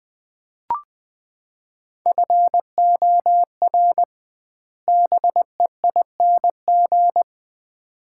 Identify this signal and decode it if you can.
Morse code, heard as a keyed tone, about 20 wpm, 700 Hz. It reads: FOR BEING
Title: 2x Word-Spacing - Sets of 4 Words - 20wpm